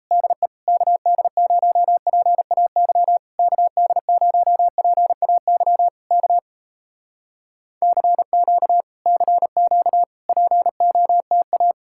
Morse code, heard as a keyed tone, about 38 wpm, 700 Hz. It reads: DE KD0PAY KD0PAY K CQ CQ POTA